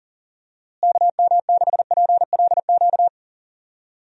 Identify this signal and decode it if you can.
Morse code, heard as a keyed tone, about 40 wpm, 700 Hz. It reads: KM6PLQ
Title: USA Call Signs - 6 Characters - Repeated 3x - 40wpm